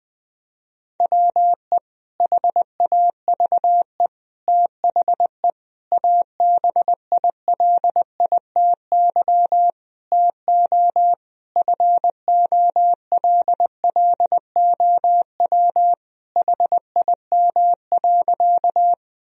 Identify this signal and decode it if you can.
Morse code, heard as a keyed tone, about 20 wpm, 700 Hz. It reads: WE HAVE THE ABILITY TO FOLLOW HIM.
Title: Sentences from Top 1000 Words - Review Part 3 - 20wpm